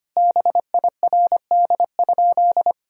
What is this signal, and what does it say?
Morse code, heard as a keyed tone, about 25 wpm, 700 Hz. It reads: BIRD?
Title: Sentences from Top 1000 Words - Review Part 2 - 25wpm